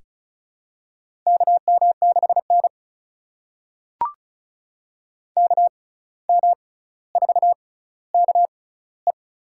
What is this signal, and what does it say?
Morse code, heard as a keyed tone, about 36 wpm, spaced as for 17 wpm, 700 Hz. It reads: KM6N KM4KE
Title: USA Call Signs - 6x Character Spacing - 35wpm